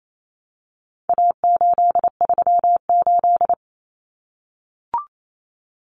Morse code, heard as a keyed tone, about 28 wpm, 700 Hz. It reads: A838
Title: Common Cut Numbers - 4-Digit 28wpm